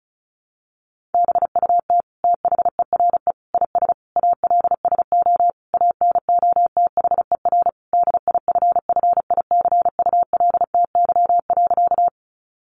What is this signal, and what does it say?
Morse code, heard as a keyed tone, about 34 wpm, 700 Hz. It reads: BUT THERE IS ALSO ANOTHER DIFFICULTY.
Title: Sentences from Top 1500 Words 35wpm